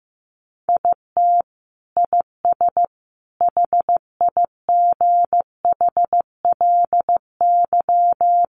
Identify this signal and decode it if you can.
Morse code, heard as a keyed tone, about 14 wpm, 700 Hz. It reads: IT IS HIGHLY
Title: Sentences from Top 1300 Words 15wpm